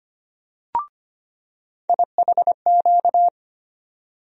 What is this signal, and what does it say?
Morse code, heard as a keyed tone, about 25 wpm, 700 Hz. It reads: IHQ
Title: USA Call Sign Postfixes - 3 Letters 25wpm